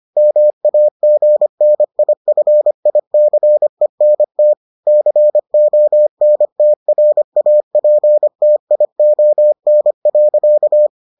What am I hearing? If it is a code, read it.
Morse code, heard as a keyed tone, about 25 wpm, 600 Hz. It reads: MAGNIFICENT CONTRAPTION.